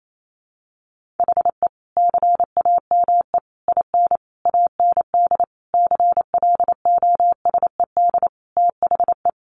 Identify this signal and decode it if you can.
Morse code, heard as a keyed tone, about 28 wpm, 700 Hz. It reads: HE CAME IN AND CLOSED THE
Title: Sentences from Top 1400 Words 28wpm